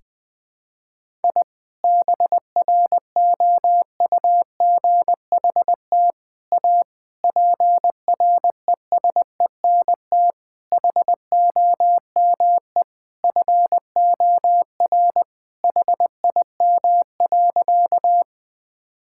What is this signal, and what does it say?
Morse code, heard as a keyed tone, about 20 wpm, 700 Hz. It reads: I BROUGHT A PRESENT HOME FOR HIM.
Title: Sentences from Top 1000 Words - Review Part 5 - 20wpm